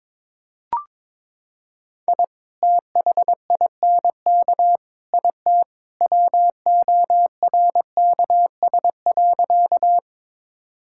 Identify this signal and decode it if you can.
Morse code, heard as a keyed tone, about 22 wpm, 700 Hz. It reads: I THINK IT WORKS.
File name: practice / 22wpm / Sentences with 2-Word Phrases - 22wpm